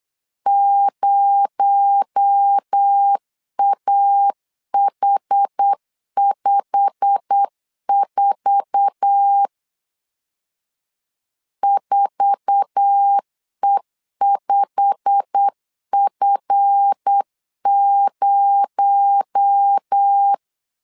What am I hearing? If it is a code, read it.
Morse code, heard as a keyed tone, about 8 wpm, 800 Hz. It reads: 0AH54 4E5F0
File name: LEZIONI / 8 YJ45 / 8 YJ45 8wpm 3214